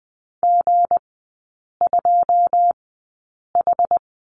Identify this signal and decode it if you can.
Morse code, heard as a keyed tone, about 20 wpm, spaced as for 8 wpm, 690 Hz. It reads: G2H
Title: 159 Lesson 26e - Letter g - Three Characters 20wpm